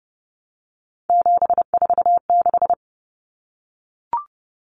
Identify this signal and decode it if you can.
Morse code, heard as a keyed tone, about 30 wpm, 700 Hz. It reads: 746
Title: Common Cut Numbers - 3-Digit 30wpm